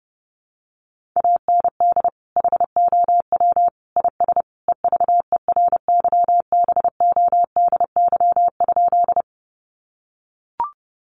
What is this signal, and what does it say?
Morse code, heard as a keyed tone, about 30 wpm, 700 Hz. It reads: AND HOW IS EVERYBODY?
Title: Sentences from Top 1400 Words 30wpm